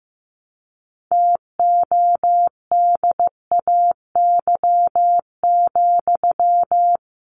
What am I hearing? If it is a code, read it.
Morse code, heard as a keyed tone, about 15 wpm, 700 Hz. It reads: TODAY,
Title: Sentences from Top 600 Words 15wpm